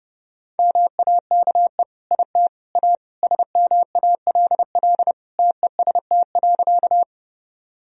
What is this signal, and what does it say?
Morse code, heard as a keyed tone, about 30 wpm, 700 Hz. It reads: MAKE IT A SMALL TEST.
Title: Sentences from Top 1000 Words - Review Part 3 - 30wpm